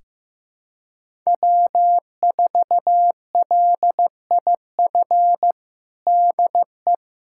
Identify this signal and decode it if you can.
Morse code, heard as a keyed tone, about 15 wpm, 700 Hz. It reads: W4LIF DE
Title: USA Callsign DE Callsign - 5 Characters - 15wpm